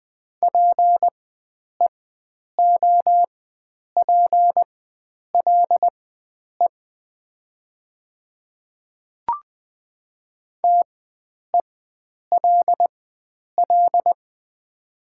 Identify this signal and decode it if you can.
Morse code, heard as a keyed tone, about 20 wpm, spaced as for 9 wpm, 700 Hz. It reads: PEOPLE TELL
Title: 4x Character Spacing - Top 300 Words - 20wpm